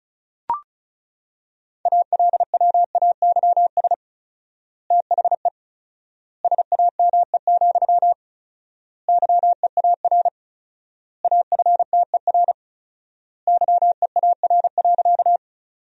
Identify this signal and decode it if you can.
Morse code, heard as a keyed tone, about 36 wpm, 700 Hz. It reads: ALWAYS THE SAME, YEAR AFTER YEAR.
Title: 4x Word-Spacing - Sentences from Top 300 Words - 35wpm